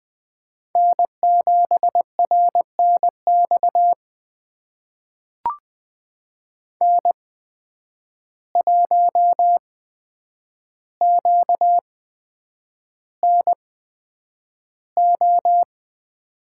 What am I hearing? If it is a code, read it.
Morse code, heard as a keyed tone, about 20 wpm, spaced as for 9 wpm, 700 Hz. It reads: N7RNX N1QNO
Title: USA Call Signs - 5 Characters - 8x Character Spacing - 20wpm